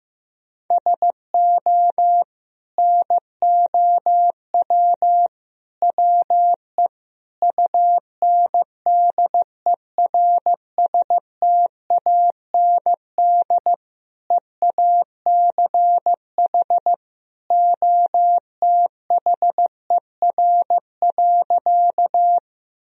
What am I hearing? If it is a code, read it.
Morse code, heard as a keyed tone, about 15 wpm, 700 Hz. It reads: SO NOW WE UNDERSTAND EACH OTHER.
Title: Sentences from Top 1000 Words - Review Part 1 - 15wpm